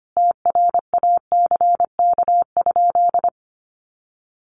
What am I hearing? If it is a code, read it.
Morse code, heard as a keyed tone, about 25 wpm, 700 Hz. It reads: TRACK?
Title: Sentences from Top 1200 Words 25wpm